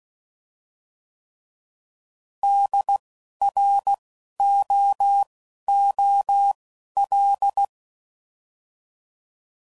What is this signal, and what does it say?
Morse code, heard as a keyed tone, about 16 wpm, spaced as for 11 wpm, 800 Hz. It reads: DROOL
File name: Prefixes / By First Letter / Ds